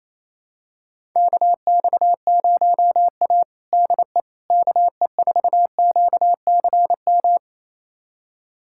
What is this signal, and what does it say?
Morse code, heard as a keyed tone, about 28 wpm, 700 Hz. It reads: KX0A DE KE4QCM
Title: USA Callsign DE Callsign 28wpm